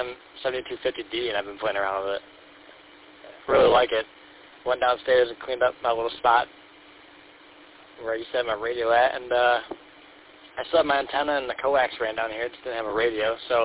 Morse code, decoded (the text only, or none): none